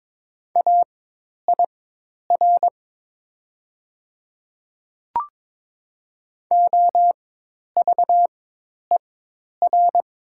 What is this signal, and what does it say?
Morse code, heard as a keyed tone, about 22 wpm, spaced as for 10 wpm, 700 Hz. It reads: AIR OVER
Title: 4x Character Spacing - Top 300 Words - 22wpm